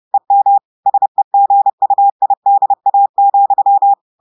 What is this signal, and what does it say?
Morse code, heard as a keyed tone, about 30 wpm, 800 Hz. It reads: EM SEGUIDA,
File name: audios / portugais / portugais texte en clair 1 vitesse 30 mots minutes